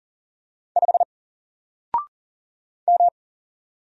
Morse code, heard as a keyed tone, about 40 wpm, 700 Hz. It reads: 5 M
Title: Single Letter-Number 40wpm